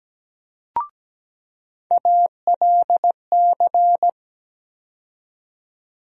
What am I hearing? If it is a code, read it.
Morse code, heard as a keyed tone, about 17 wpm, 700 Hz. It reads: ALC
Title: USA Call Sign Postfixes - 3 Letters 17wpm